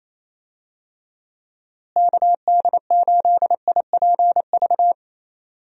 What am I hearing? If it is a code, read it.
Morse code, heard as a keyed tone, about 28 wpm, 700 Hz. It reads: KD8IPV
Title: USA Call Signs - 6 Characters - Repeated 2x - 28wpm